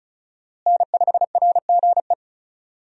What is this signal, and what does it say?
Morse code, heard as a keyed tone, about 36 wpm, 700 Hz. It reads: N5RGE
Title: USA Call Signs - 5 Characters - Repeated 2x - 35wpm